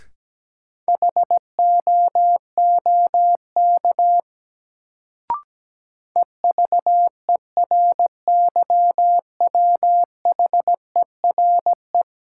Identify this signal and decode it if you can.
Morse code, heard as a keyed tone, about 17 wpm, 700 Hz. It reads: HOOK EVERYWHERE
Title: Sentences from Top 1600 Words - New Word Drill - 17wpm